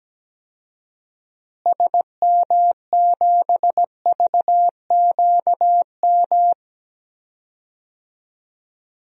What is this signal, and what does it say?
Morse code, heard as a keyed tone, about 17 wpm, 700 Hz. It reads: SM7VQM
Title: DXCC-100 Call Signs - 6 Characters - Repeated 2x - 17wpm